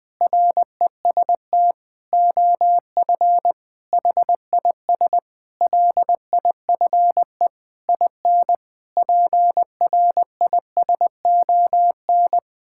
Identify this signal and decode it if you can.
Morse code, heard as a keyed tone, about 20 wpm, 700 Hz. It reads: REST OF HIS LIFE IN PRISON
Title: Sets of 8 Words - 20wpm